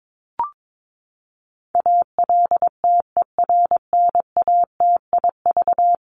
Morse code, heard as a keyed tone, about 22 wpm, 700 Hz. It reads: ALTERNATIV
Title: Top 501-750 Words 22wpm